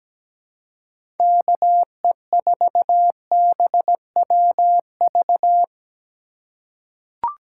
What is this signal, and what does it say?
Morse code, heard as a keyed tone, about 17 wpm, 700 Hz. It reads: KE4BWV
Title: USA Call Signs - 6 Characters - 3x Character Spacing - 17wpm